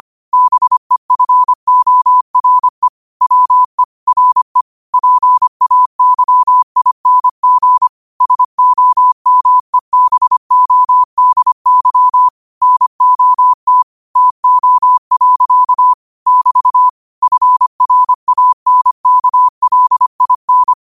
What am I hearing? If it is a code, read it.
Morse code, heard as a keyed tone, about 25 wpm, 1000 Hz. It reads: BEFORE WE RE PAYING SOMEBODY NOT TO. <BT> FRANKLIN